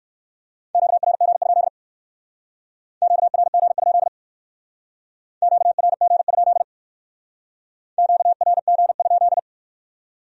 Morse code, heard as a keyed tone, about 53 wpm, 700 Hz. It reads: QRG? QRG? QRG? QRG?
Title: Speed-Racing Q-Codes - 45wpm